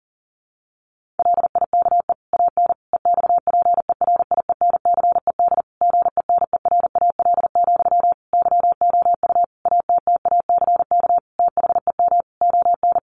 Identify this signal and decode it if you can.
Morse code, heard as a keyed tone, about 40 wpm, 700 Hz. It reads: LIKE AN EXPERIENCED GENERAL, YOU ATTACK THEM ON
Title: Sentences from Top 1500 Words 40wpm